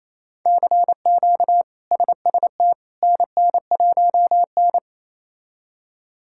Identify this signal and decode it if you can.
Morse code, heard as a keyed tone, about 28 wpm, 700 Hz. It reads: CQ SST NN1N
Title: K1USN SST QSO Practice - 28wpm